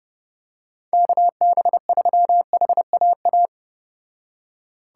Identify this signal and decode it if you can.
Morse code, heard as a keyed tone, about 29 wpm, 700 Hz. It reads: KB3HAA